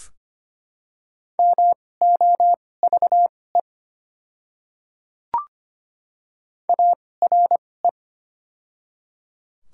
Morse code, heard as a keyed tone, about 25 wpm, spaced as for 18 wpm, 700 Hz. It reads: MOVE ARE
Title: 2x Character Spacing - Top 200 Words - 25wpm